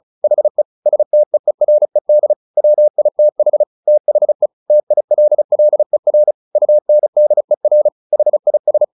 Morse code, heard as a keyed tone, about 34 wpm, 600 Hz. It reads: HE STEERED WITH THE TILLER UNDER HIS
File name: the old man and the sea 2 35wpm